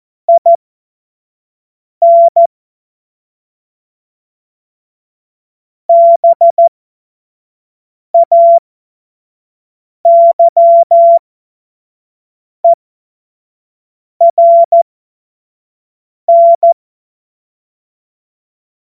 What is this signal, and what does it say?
Morse code, heard as a keyed tone, about 14 wpm, spaced as for 5 wpm, 680 Hz. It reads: IN BAYERN
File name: 25-F70